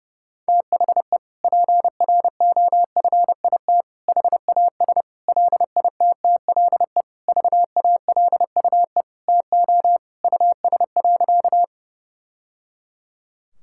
Morse code, heard as a keyed tone, about 30 wpm, 700 Hz. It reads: THE PROFIT HAS LITTLE VALUE TO US.
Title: Sentences from Top 1000 Words - Review Part 5 - 30wpm